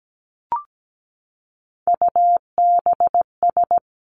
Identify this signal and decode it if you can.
Morse code, heard as a keyed tone, about 17 wpm, 700 Hz. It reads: UBS
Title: Three Challenging Letter-Number Combinations 17wpm